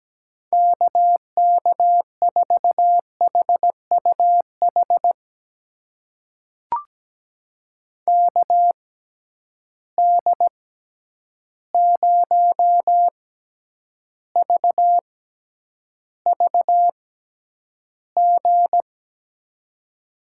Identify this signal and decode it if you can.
Morse code, heard as a keyed tone, about 17 wpm, spaced as for 9 wpm, 700 Hz. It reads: KK4HUH KD0VVG
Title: USA Call Signs - 6 Characters - 6x Character Spacing - 17wpm